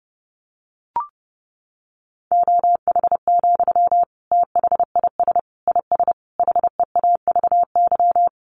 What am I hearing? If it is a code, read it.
Morse code, heard as a keyed tone, about 30 wpm, 700 Hz. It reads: OH, THIS IS HEAVY
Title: Sentences from Top 1100 Words 30wpm